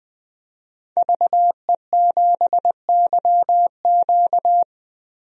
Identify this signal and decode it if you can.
Morse code, heard as a keyed tone, about 20 wpm, 700 Hz. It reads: VE7YQ
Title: DXCC-100 Call Signs - 5 Characters - 20wpm